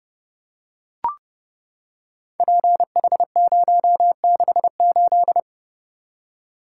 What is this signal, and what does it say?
Morse code, heard as a keyed tone, about 30 wpm, 700 Hz. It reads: PH068
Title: Code Groups - 5 Letter:Numbers - 30wpm